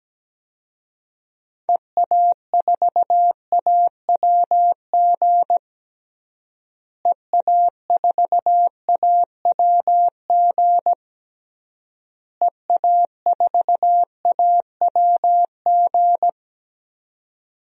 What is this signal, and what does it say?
Morse code, heard as a keyed tone, about 17 wpm, 700 Hz. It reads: EA4AWG EA4AWG EA4AWG
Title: DXCC-100 Call Signs - 6 Characters - Repeated 3x - 17wpm